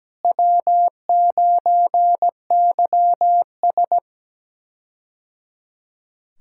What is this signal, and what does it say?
Morse code, heard as a keyed tone, about 17 wpm, 700 Hz. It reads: W9YS
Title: K1USN SST QSO Practice - 17wpm